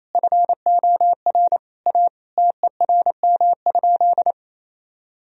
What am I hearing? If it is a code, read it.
Morse code, heard as a keyed tone, about 28 wpm, 700 Hz. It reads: FOR A TERM?
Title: Sentences from Top 1000 Words - Review Part 4 - 28wpm